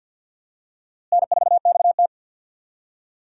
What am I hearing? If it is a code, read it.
Morse code, heard as a keyed tone, about 52 wpm, 700 Hz. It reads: N4XT
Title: USA Call Signs - 4 Characters - Repeated 2x - 50wpm